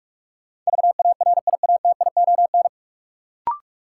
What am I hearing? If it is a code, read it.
Morse code, heard as a keyed tone, about 42 wpm, 700 Hz. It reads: VARIATION
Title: Sentences from Top 1600 Words - New Word Drill - 45wpm